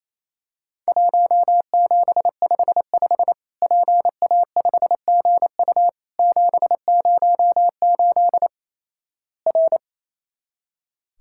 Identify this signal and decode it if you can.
Morse code, heard as a keyed tone, about 28 wpm, 700 Hz. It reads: 1755 PA5GU 708 R
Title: Worked All Europe DX Contest - QTC Exchange - 28wpm